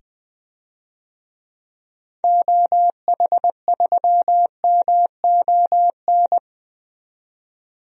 Morse code, heard as a keyed tone, about 20 wpm, 700 Hz. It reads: OH3MON
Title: DXCC-100 Call Signs - 6 Characters - 20wpm